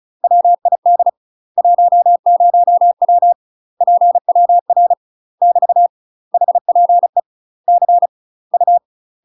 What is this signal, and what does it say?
Morse code, heard as a keyed tone, about 36 wpm, 700 Hz. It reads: WID 10W PWR <BT> HPE C U
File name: UK MASTER QSOs TRAINING / 35WPM / UK MASTER QSOs 0032